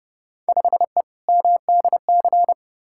Morse code, heard as a keyed tone, about 30 wpm, 700 Hz. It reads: 5E MDC
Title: Field Day Exchanges - 30wpm